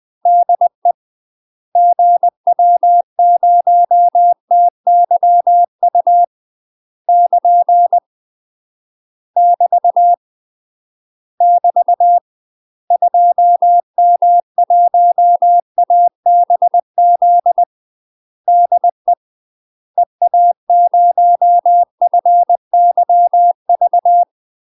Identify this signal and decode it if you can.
Morse code, heard as a keyed tone, about 20 wpm, 700 Hz. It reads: DE GW0TYU <KN> <BT> <BT> 2M1ABZ DE EA0FYV
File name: UK MASTER QSOs TRAINING / 20WPM / UK MASTER QSOs 0041